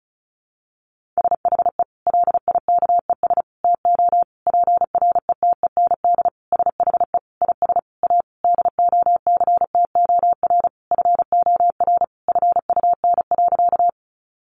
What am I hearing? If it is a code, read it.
Morse code, heard as a keyed tone, about 34 wpm, 700 Hz. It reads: SHE LIKES TO PRETEND SHE IS A DOCTOR FOR FUN.
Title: Additional Easy Phrases 35wpm